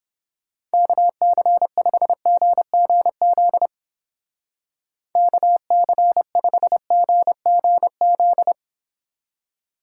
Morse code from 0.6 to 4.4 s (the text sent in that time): KC5GGZ